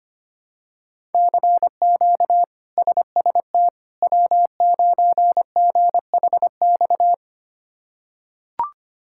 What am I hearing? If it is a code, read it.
Morse code, heard as a keyed tone, about 25 wpm, 700 Hz. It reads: CQ SST W9GHX
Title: K1USN SST QSO Practice - 25wpm